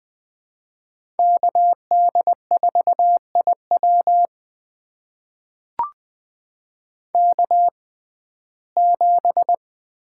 Morse code, heard as a keyed tone, about 20 wpm, spaced as for 14 wpm, 700 Hz. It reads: KD4IW K7
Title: USA Call Signs - 5 Characters - 6x Character Spacing - 20wpm